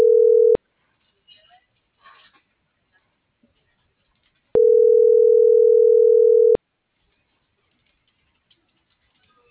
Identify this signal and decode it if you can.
Ringback tone (North America), 440 + 480 Hz.